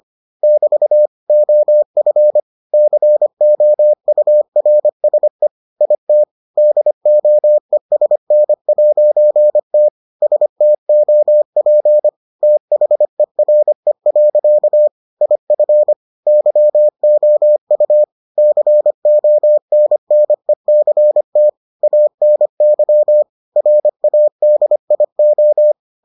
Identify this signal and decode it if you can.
Morse code, heard as a keyed tone, about 25 wpm, 600 Hz. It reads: <BT> OF COURSE IT DOESN'T STOP THERE. IF YOU CONNECT ANY RADIO